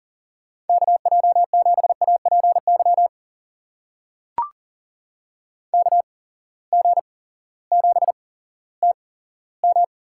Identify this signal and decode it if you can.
Morse code, heard as a keyed tone, about 40 wpm, spaced as for 18 wpm, 700 Hz. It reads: KJ7APY KG7TM
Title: USA Call Signs - 8x Character Spacing - 40wpm